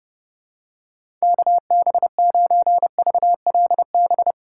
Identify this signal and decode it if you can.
Morse code, heard as a keyed tone, about 32 wpm, 700 Hz. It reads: KB9VLB
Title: USA Call Signs - 3x Character Spacing - 30wpm